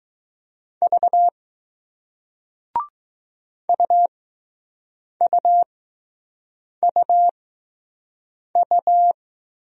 Morse code, heard as a keyed tone, about 20 wpm, spaced as for 6 wpm, 700 Hz. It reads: V UUUU